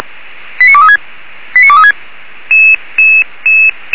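Five-tone selective call (ZVEI), heard as quick runs of five tones, tones of about 70 ms. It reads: 89236 89236
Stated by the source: ZVEI five-tone selective call received by SDR.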